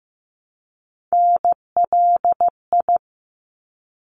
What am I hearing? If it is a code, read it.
Morse code, heard as a keyed tone, about 15 wpm, 700 Hz. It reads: NLI
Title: ARRL Sections 15wpm